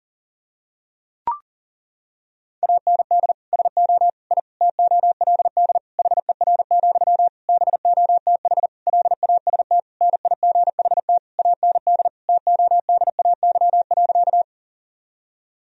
Morse code, heard as a keyed tone, about 40 wpm, 700 Hz. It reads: AND SO I TOLD HER, BOTH LAST NIGHT AND TODAY.
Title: Sentences from Top 1000 Words - Review Part 4 - 40wpm